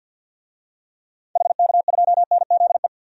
Morse code, heard as a keyed tone, about 53 wpm, 700 Hz. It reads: HK2NZE
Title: DXCC-100 Call Signs - 6 Characters - 50wpm